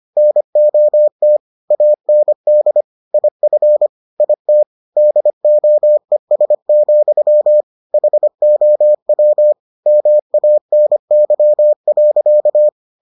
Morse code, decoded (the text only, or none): NOT AND IF IT DOES, HOW MANY.